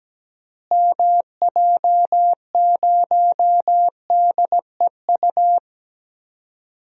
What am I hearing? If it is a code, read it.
Morse code, heard as a keyed tone, about 17 wpm, 700 Hz. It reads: MJ0DEU
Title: DXCC-100 Call Signs - 6 Characters - Repeated 2x - 17wpm